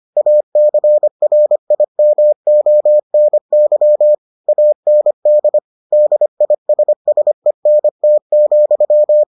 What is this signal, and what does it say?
Morse code, heard as a keyed tone, about 25 wpm, 600 Hz. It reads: ACRIMONY AND DISSENT,